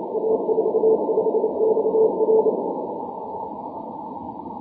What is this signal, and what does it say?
Morse code, heard as a keyed tone, about 29 wpm, 440 Hz, with noise. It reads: A45XR
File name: wp / a45xr 6m cw 2 jun 2011 1602z